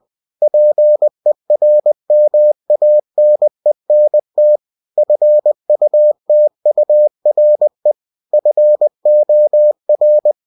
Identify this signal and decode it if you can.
Morse code, heard as a keyed tone, about 20 wpm, 600 Hz. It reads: PERMANENT FUTURE FOR
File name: cw-books / winnie-the-pooh / winnie-20wpm-page-0120